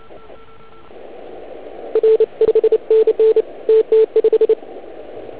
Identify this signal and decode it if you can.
Morse code, heard as a keyed tone, about 29 wpm, 440 Hz, with radio noise. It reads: R 5NN TT5